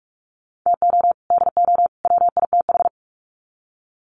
Morse code, heard as a keyed tone, about 44 wpm, 700 Hz. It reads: TO DO WITH
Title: Sets of 3 Words 45wpm